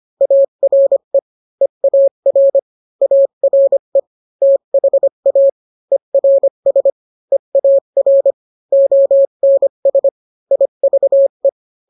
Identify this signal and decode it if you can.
Morse code, heard as a keyed tone, about 25 wpm, 550 Hz. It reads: ARE EAR ARE THA ERS EAR ONS IVE